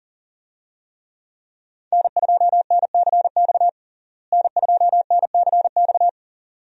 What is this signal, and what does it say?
Morse code, heard as a keyed tone, about 40 wpm, 700 Hz. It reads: N2NCX N2NCX
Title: USA Call Signs - 5 Characters - Repeated 2x - 40wpm